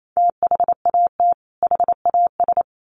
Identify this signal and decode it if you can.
Morse code, heard as a keyed tone, about 28 wpm, 700 Hz. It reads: THAT HAS
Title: Sets of 2 Words 28wpm